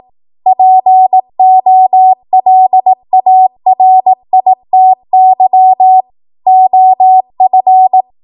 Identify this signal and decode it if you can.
Morse code, heard as a keyed tone, about 18 wpm, 750 Hz. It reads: POLARITY OF
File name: Morse / 18WPM